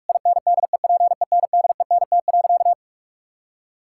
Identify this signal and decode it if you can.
Morse code, heard as a keyed tone, about 46 wpm, 700 Hz. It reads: INDEPENDENT.
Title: Sentences from Top 1000 Words - Review Part 6 - 45wpm